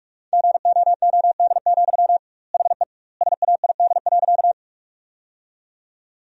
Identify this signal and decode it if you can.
Morse code, heard as a keyed tone, about 41 wpm, 700 Hz. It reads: GOOD, HE SAID.
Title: 2x Word-Spacing - Sentences from Top 200 Words - 45wpm